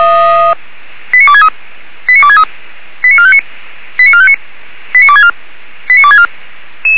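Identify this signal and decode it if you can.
Five-tone selective call (ZVEI), heard as quick runs of five tones, tones of about 80 ms. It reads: 89263 89263 89459 89459 89264 89264
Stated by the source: ZVEI five-tone selective call received by SDR.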